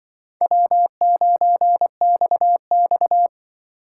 Morse code, heard as a keyed tone, about 24 wpm, 700 Hz. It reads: W9XX
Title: Speed Racing - USA 1x2 Call Signs 20wpm